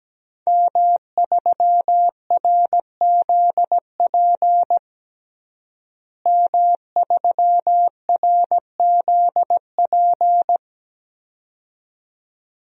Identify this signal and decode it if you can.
Morse code, heard as a keyed tone, about 17 wpm, 700 Hz. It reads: M3RZP M3RZP
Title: DXCC-100 Call Signs - 5 Characters - Repeated 2x - 17wpm